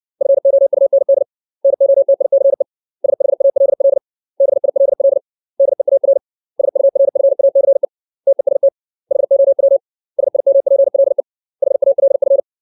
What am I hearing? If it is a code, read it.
Morse code, heard as a keyed tone, about 60 wpm, 550 Hz. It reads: FOUND NOTICE SHALL BILL DEAR SURFACE TEST HOW SIMPLE HALF
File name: 632 words / 632 Words 60WPM